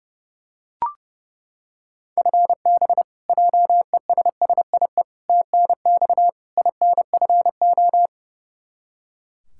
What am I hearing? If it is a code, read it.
Morse code, heard as a keyed tone, about 30 wpm, 700 Hz. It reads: FB JESSIE TNX INFO
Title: Practice QSO - Exchange 3 - More Info - 30wpm